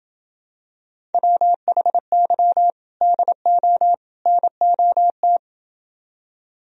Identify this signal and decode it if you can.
Morse code, heard as a keyed tone, about 28 wpm, 700 Hz. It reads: WHY DO NOT